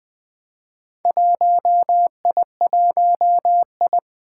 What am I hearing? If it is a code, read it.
Morse code, heard as a keyed tone, about 20 wpm, 700 Hz. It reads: 1I1I
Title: Four Challenging Letter-Number Combinations 20wpm